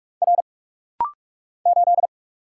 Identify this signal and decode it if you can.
Morse code, heard as a keyed tone, about 44 wpm, 700 Hz. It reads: R 8
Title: Single Letter-Number Alt - Rapid-Fire 45wpm